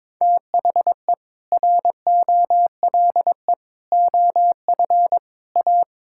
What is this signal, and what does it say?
Morse code, heard as a keyed tone, about 22 wpm, 700 Hz. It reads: THE ROLE OF A
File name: practice / 22wpm / Sentences with 3-Word Phrases - 22wpm